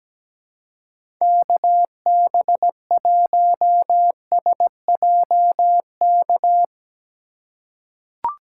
Morse code, heard as a keyed tone, about 17 wpm, 700 Hz. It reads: KB1SJK
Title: USA Call Signs - 6 Characters - 4x Character Spacing - 17wpm